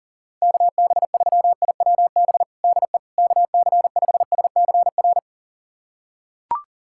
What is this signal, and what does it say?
Morse code, heard as a keyed tone, about 37 wpm, 700 Hz. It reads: KB3IWB DE KC5SCR
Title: USA Callsign DE Callsign - 6 Characters - 40wpm